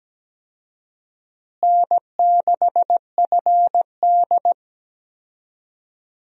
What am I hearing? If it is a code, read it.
Morse code, heard as a keyed tone, about 17 wpm, 700 Hz. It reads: N6FD
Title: USA Call Signs - 4 Characters - Repeated 2x - 17wpm